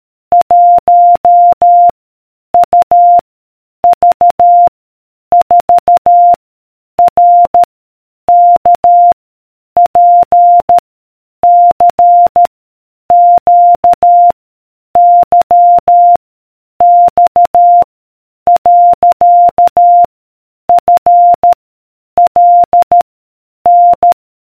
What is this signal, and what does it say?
Morse code, heard as a keyed tone, about 13 wpm, 700 Hz. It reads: 1 U V 4 R K P C Q Y X . F L N